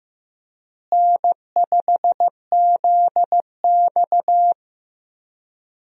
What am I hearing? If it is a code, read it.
Morse code, heard as a keyed tone, about 15 wpm, 700 Hz. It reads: N5ZX
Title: USA Call Signs - 4 Characters - Repeated 3x - 15wpm